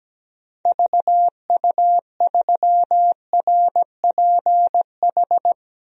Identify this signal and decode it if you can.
Morse code, heard as a keyed tone, about 17 wpm, 700 Hz. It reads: VU3RPH
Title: DXCC-100 Call Signs - 6 Characters - 17wpm